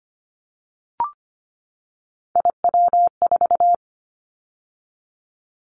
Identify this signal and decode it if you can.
Morse code, heard as a keyed tone, about 25 wpm, 700 Hz. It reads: IW4
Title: Callsign Prefixes for Top 50 DXCC - 25wpm